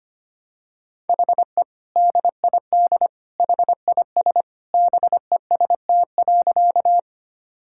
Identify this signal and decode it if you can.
Morse code, heard as a keyed tone, about 25 wpm, 700 Hz. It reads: HE DID HIS BEST.
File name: practice / 25wpm / Sentences with 2-Word Phrases - 25wpm